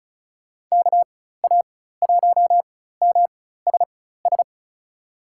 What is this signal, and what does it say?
Morse code, heard as a keyed tone, about 36 wpm, spaced as for 16 wpm, 700 Hz. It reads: KA1MSS
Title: USA Call Signs - 4x Character Spacing - 35wpm